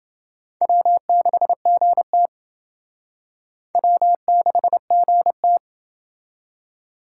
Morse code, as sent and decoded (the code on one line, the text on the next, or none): .-- -.... --. - / .-- -.... --. -
W6GT W6GT